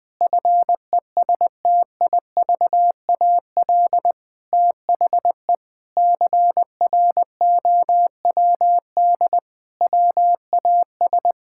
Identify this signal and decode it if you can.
Morse code, heard as a keyed tone, about 20 wpm, 700 Hz. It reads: FESTIVAL THE CROWD WAS
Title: Sentences with 6-Word Phrases - 20wpm